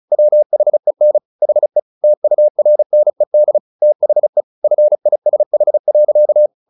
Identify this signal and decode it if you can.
Morse code, heard as a keyed tone, about 34 wpm, 600 Hz. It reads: WHEN HE TURNED THE FISH.
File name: the old man and the sea 2 35wpm